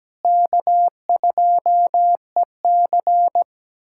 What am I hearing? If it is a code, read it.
Morse code, heard as a keyed tone, about 17 wpm, 700 Hz. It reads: K2EC
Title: USA Callsign DE Callsign - 4 Characters - 17wpm